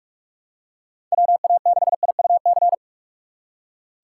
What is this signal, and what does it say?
Morse code, heard as a keyed tone, about 45 wpm, 700 Hz. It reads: WA6IUC
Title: USA Call Signs - 6 Characters - 4x Character Spacing - 45wpm